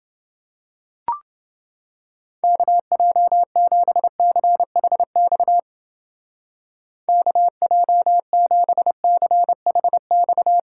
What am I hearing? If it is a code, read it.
Morse code, heard as a keyed tone, about 29 wpm, 700 Hz. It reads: KJ7CHX KJ7CHX